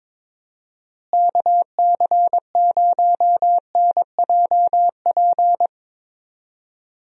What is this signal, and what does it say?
Morse code, heard as a keyed tone, about 22 wpm, 700 Hz. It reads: KC0NJP
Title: USA Call Signs - 6 Characters - Repeated 2x - 22wpm